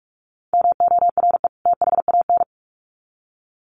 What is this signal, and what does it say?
Morse code, heard as a keyed tone, about 44 wpm, 700 Hz. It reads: MORE THAN
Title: Sets of 2 Words 45wpm